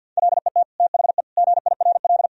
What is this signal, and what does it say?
Morse code, heard as a keyed tone, about 52 wpm, 700 Hz. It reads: LET THE GIRL
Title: Sentences from Top 1000 Words - Review Part 1 - 50wpm